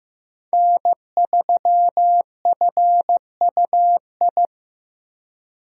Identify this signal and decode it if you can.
Morse code, heard as a keyed tone, about 15 wpm, 700 Hz. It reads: N3FUI
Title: USA Call Signs - 5 Characters - Repeated 2x - 15wpm